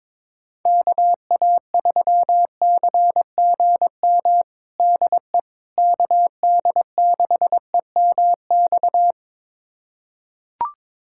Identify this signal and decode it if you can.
Morse code, heard as a keyed tone, about 22 wpm, 700 Hz. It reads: KA3CGM DE KD6EMX
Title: USA Callsign DE Callsign 22wpm